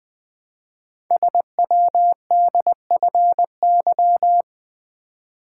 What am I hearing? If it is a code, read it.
Morse code, heard as a keyed tone, about 20 wpm, 700 Hz. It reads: SWDFY
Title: Code Groups - 5 Letters - 20wpm